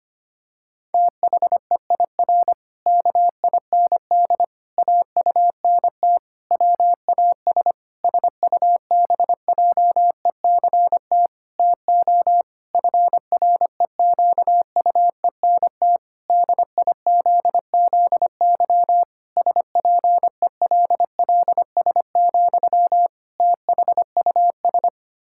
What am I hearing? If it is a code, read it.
Morse code, heard as a keyed tone, about 25 wpm, 700 Hz. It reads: THEIR KIND AUNT WAS SUBJECT TO FREQUENT DIZZY SPELLS, THUS
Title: Pangram Challenge v2 - 25wpm